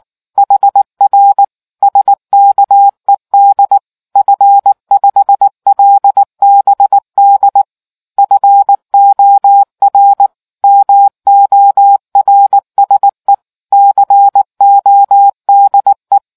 Morse code, heard as a keyed tone, about 19 wpm, 800 Hz. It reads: HR SKED F5LBD FOR MORSE CODE